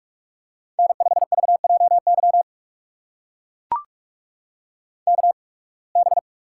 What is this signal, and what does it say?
Morse code, heard as a keyed tone, about 41 wpm, spaced as for 28 wpm, 700 Hz. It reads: N5VJY KB
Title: USA Call Signs - 5 Characters - 8x Character Spacing - 45wpm